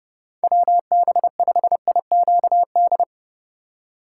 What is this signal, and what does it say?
Morse code, heard as a keyed tone, about 32 wpm, 700 Hz. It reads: WB5IQD
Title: USA Call Signs - 6 Characters - 4x Character Spacing - 30wpm